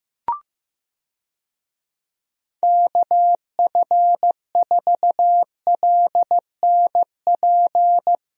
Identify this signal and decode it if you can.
Morse code, heard as a keyed tone, about 15 wpm, 700 Hz. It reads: KF4LNP
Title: USA Call Signs - 6 Characters - Repeated 3x - 15wpm